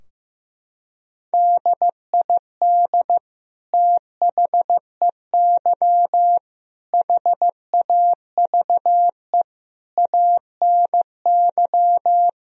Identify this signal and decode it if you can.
Morse code, heard as a keyed tone, about 15 wpm, 700 Hz. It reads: DID THEY HAVE ANY
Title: Sentences from Top 1000 Words - Review Part 3 - 15wpm